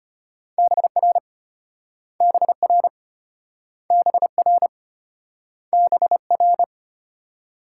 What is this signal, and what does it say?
Morse code, heard as a keyed tone, about 34 wpm, 700 Hz. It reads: BR BR BR BR